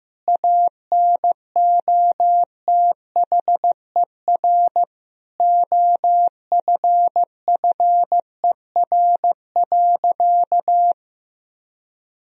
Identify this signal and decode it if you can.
Morse code, heard as a keyed tone, about 15 wpm, 700 Hz. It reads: ANOTHER OFFER.
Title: Sentences from Top 1000 Words - Review Part 4 - 15wpm